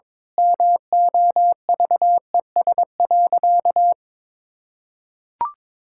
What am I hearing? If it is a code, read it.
Morse code, heard as a keyed tone, about 22 wpm, 700 Hz. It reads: MOVES.
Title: Pangram Challenge v2 - 22wpm